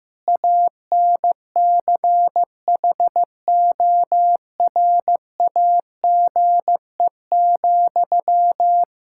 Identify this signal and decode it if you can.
Morse code, heard as a keyed tone, about 15 wpm, 700 Hz. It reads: ANCHORAGE,
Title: Practice QSO - Exchange 1 - Beginning - 15wpm